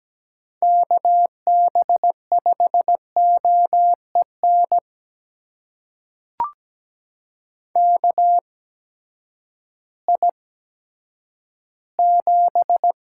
Morse code, heard as a keyed tone, about 17 wpm, spaced as for 10 wpm, 700 Hz. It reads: KB5OEN KI7
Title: USA Call Signs - 6 Characters - 8x Character Spacing - 17wpm